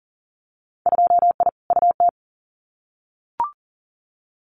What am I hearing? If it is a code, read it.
Morse code, heard as a keyed tone, about 41 wpm, 700 Hz. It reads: 2I UT
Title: Winter Field Day Exchanges 40wpm